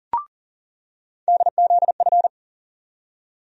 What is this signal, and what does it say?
Morse code, heard as a keyed tone, about 40 wpm, 700 Hz. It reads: DZF
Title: USA Call Sign Postfixes - 3 Letters 40wpm